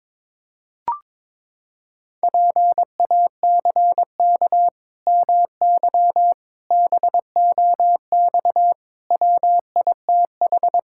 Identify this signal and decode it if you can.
Morse code, heard as a keyed tone, about 21 wpm, 700 Hz. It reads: PACK MY BOX WITH
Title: Pangram Challenge v2 - 22wpm